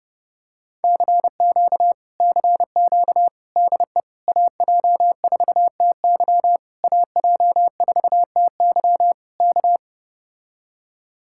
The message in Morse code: CQ CQ DE AJ4TY AJ4TY K

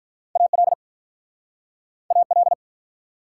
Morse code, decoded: AL AL

